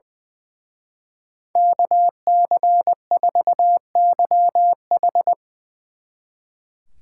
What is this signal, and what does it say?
Morse code, heard as a keyed tone, about 20 wpm, 700 Hz. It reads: KC4YH
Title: Speed Racing - USA 2x2 Call Signs 20wpm